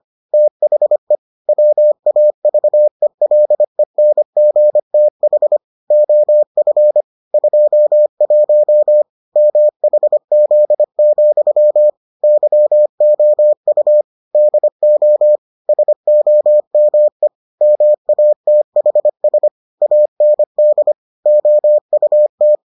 Morse code, heard as a keyed tone, about 25 wpm, 600 Hz. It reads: THE WAVELENGTH OF 21 MHZ, YOU DO SOME MATHS AND OUT